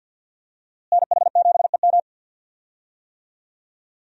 Morse code, heard as a keyed tone, about 50 wpm, 700 Hz. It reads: NH7EM
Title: USA Call Signs - 5 Characters - Repeated 3x - 50wpm